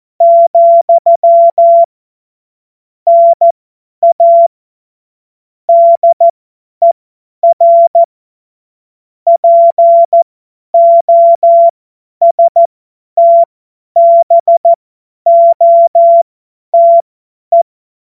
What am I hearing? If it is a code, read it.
Morse code, heard as a keyed tone, about 14 wpm, spaced as for 10 wpm, 680 Hz. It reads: , NA DER POSTBOTE
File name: Morsekurs / Witze / Witz 02. Februar 2021, 50 F70